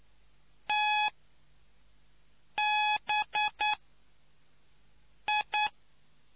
Morse code, heard as a keyed tone, about 9 wpm, spaced as for 4 wpm, 840 Hz. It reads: TBI